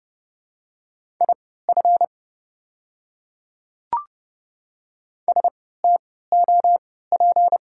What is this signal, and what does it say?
Morse code, heard as a keyed tone, about 31 wpm, spaced as for 17 wpm, 700 Hz. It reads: IF STOP
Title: 3x Character Spacing - Top 500 Words - 30wpm